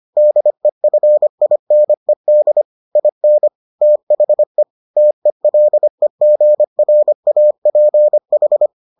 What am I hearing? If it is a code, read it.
Morse code, heard as a keyed tone, about 25 wpm, 600 Hz. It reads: DEFINED IN THE TELEGRAPH